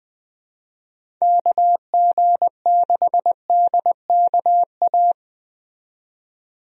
Morse code, heard as a keyed tone, about 20 wpm, 700 Hz. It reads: KG6DKA
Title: USA Call Signs - 6 Characters - 20wpm